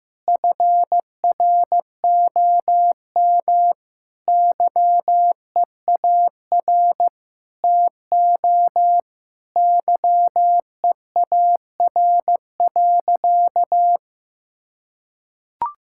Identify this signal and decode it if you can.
Morse code, heard as a keyed tone, about 15 wpm, 700 Hz. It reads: FROM YEAR TO YEAR.
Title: Sentences from Top 1600 Words 15wpm